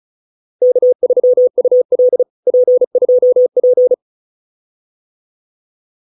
Morse code, heard as a keyed tone, about 34 wpm, 500 Hz. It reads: K3UL P2P